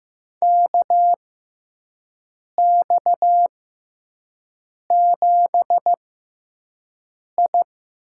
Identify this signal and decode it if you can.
Morse code, heard as a keyed tone, about 15 wpm, spaced as for 5 wpm, 700 Hz. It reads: KX7I
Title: USA Call Signs - 4 Characters - 6x Character Spacing - 15wpm